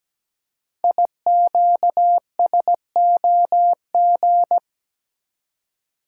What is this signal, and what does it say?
Morse code, heard as a keyed tone, about 17 wpm, 700 Hz. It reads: IQSOG
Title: Code Groups - 5 Letters - 17wpm